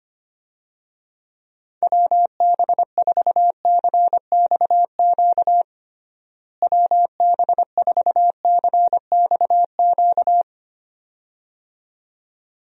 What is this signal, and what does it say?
Morse code, heard as a keyed tone, about 25 wpm, 700 Hz. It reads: WB4CXQ WB4CXQ
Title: USA Call Signs - 6 Characters - Repeated 2x - 25wpm